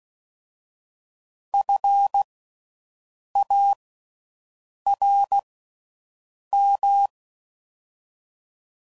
Morse code, heard as a keyed tone, about 16 wpm, 800 Hz. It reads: F A R M
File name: Suffixes / General / Houses